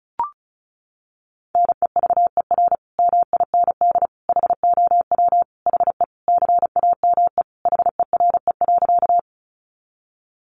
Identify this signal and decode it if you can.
Morse code, heard as a keyed tone, about 34 wpm, 700 Hz. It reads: NEVER MIND HOW HE CAME HERE.
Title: Sentences from Top 500 Words 35wpm